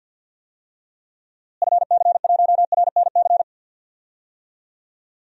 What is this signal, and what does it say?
Morse code, heard as a keyed tone, about 50 wpm, 700 Hz. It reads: FC1RNC
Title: DXCC-100 Call Signs - 6 Characters - Repeated 3x - 50wpm